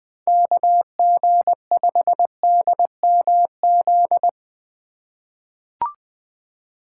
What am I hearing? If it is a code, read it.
Morse code, heard as a keyed tone, about 20 wpm, 700 Hz. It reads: KG5DMZ
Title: USA Call Signs - 6 Characters - 4x Character Spacing - 20wpm